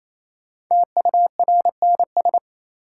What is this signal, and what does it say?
Morse code, heard as a keyed tone, about 28 wpm, 700 Hz. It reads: TURNS
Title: Sets of 5 Words Encore - New Word Drill - 28wpm